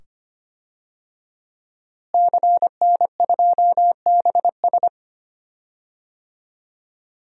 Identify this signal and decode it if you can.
Morse code, heard as a keyed tone, about 25 wpm, 700 Hz. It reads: CN2BS
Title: DXCC-100 Call Signs - 5 Characters - Repeated 2x - 25wpm